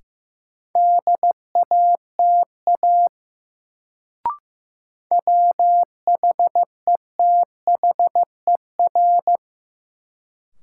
Morse code, heard as a keyed tone, about 15 wpm, 700 Hz. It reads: DATA WHETHER
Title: Sentences from Top 1000 Words - Review Part 3 - New Word Drill - 15wpm